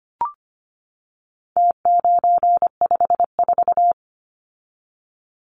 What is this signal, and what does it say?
Morse code, heard as a keyed tone, about 25 wpm, 700 Hz. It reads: T954
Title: Numbers - Contest Style - 4-Digit 25wpm